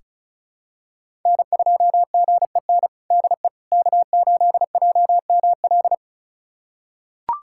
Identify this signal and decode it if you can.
Morse code, heard as a keyed tone, about 34 wpm, 690 Hz. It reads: N2GEN DE K8JML
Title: USA Callsign DE Callsign - 5 Characters - 35wpm